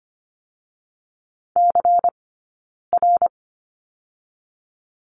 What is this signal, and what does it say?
Morse code, heard as a keyed tone, about 24 wpm, 700 Hz. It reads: C R
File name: course / 25:10wpm / 092 Lesson 16d - Letter c - Two Characters 25wpm